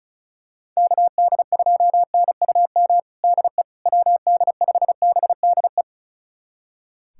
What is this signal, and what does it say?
Morse code, heard as a keyed tone, about 36 wpm, 700 Hz. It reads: KD2NUM DE WD5BDE
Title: USA Callsign DE Callsign 35wpm